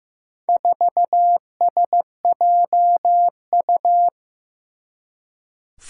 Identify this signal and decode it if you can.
Morse code, heard as a keyed tone, about 15 wpm, 700 Hz. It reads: 4SJU
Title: Four Challenging Letter-Number Combinations 15wpm